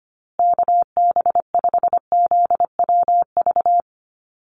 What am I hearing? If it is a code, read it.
Morse code, heard as a keyed tone, about 25 wpm, 700 Hz. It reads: KB5ZWV
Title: USA Call Signs Repeated 3x 25wpm